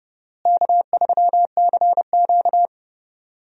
Morse code, heard as a keyed tone, about 30 wpm, 700 Hz. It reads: K3CQ
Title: USA Call Signs - 4 Characters - Repeated 3x - 30wpm